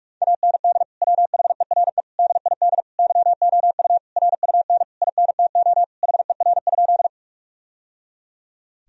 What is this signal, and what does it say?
Morse code, heard as a keyed tone, about 46 wpm, 700 Hz. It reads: AND WHERE DID YOU RUN INTO HER?
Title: Sentences from Top 1000 Words - Review Part 1 - 45wpm